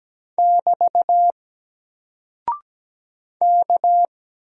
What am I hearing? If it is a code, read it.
Morse code, heard as a keyed tone, about 17 wpm, 700 Hz. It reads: <BT> K
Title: Single Character (Letter:Number:Punctuation:Prosign) Alt 17wpm